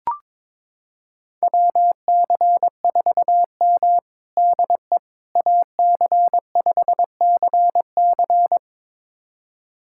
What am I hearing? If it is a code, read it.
Morse code, heard as a keyed tone, about 22 wpm, 700 Hz. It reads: WC4M DE AC5CC
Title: USA Callsign DE Callsign 22wpm